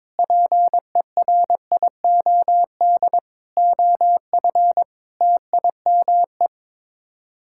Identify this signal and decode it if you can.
Morse code, heard as a keyed tone, about 22 wpm, 700 Hz. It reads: PERIOD OF TIME
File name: practice / 22wpm / Sets of 6 Words - 22wpm